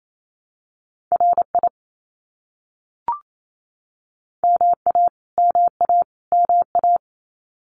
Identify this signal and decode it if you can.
Morse code, heard as a keyed tone, about 28 wpm, 700 Hz. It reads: RI MA MA MA
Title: US States Repeated 3x 28wpm